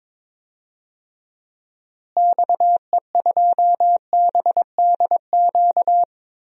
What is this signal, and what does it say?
Morse code, heard as a keyed tone, about 22 wpm, 700 Hz. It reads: XE2BDQ
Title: DXCC-100 Call Signs - 6 Characters - Repeated 2x - 22wpm